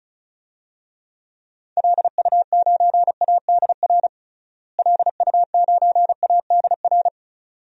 Morse code, heard as a keyed tone, about 36 wpm, 700 Hz. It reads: LU9ADR LU9ADR
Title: DXCC-100 Call Signs - 6 Characters - Repeated 2x - 35wpm